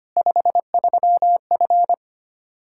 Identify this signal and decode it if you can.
Morse code, heard as a keyed tone, about 25 wpm, 700 Hz. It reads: 53F
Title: Practice QSO - Exchange 2 - Info - 25wpm